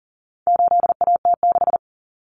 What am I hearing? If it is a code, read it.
Morse code, heard as a keyed tone, about 40 wpm, 700 Hz. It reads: 8AT6
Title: Common Cut Numbers - 4-Digit 40wpm